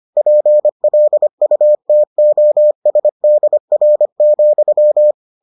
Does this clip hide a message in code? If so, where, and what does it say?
Yes: Morse code from 0.2 to 5.1 s, heard as a keyed tone, about 25 wpm, 600 Hz. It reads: PLUTOSDR,